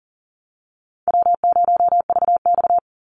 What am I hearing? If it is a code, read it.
Morse code, heard as a keyed tone, about 40 wpm, 700 Hz. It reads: W0VX
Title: Call Signs Repeated 3x - 40wpm